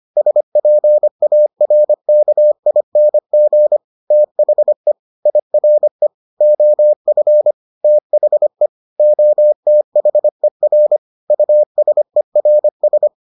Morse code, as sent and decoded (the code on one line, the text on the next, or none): ... .--. .- .-. -.- .. -. --. / - .... / .. .-. / --- ..-. / - .... / --- - .... .-. / ..- ... .-. ...
SPARKING THE IRE OF THE OTHER USERS